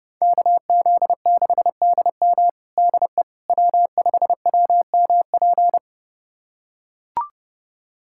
Morse code, heard as a keyed tone, about 30 wpm, 700 Hz. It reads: KZ6DM DE W5WMP
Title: USA Callsign DE Callsign - 5 Characters - 30wpm